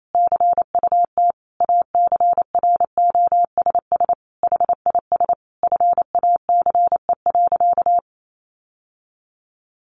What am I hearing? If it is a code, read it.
Morse code, heard as a keyed tone, about 28 wpm, 700 Hz. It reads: CUT ACROSS HIS FACE.
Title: Sentences from Top 700 Words 28wpm